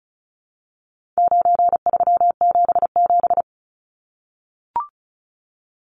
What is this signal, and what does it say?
Morse code, heard as a keyed tone, about 34 wpm, 700 Hz. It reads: 9377
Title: Numbers - 4-Digit 35wpm